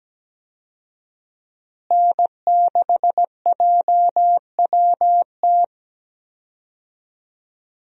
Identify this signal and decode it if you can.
Morse code, heard as a keyed tone, about 17 wpm, 700 Hz. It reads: N6JWT
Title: USA Call Signs - 5 Characters - Repeated 2x - 17wpm